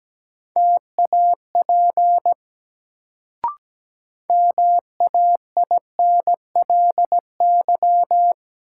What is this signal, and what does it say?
Morse code, heard as a keyed tone, about 17 wpm, 700 Hz. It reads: TAP MAINLY
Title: Sentences from Top 1400 Words - New Word Drill - 17wpm